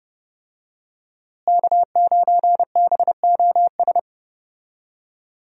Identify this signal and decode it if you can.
Morse code, heard as a keyed tone, about 30 wpm, 700 Hz. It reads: K9BOS
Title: USA Call Signs - 5 Characters - Repeated 3x - 30wpm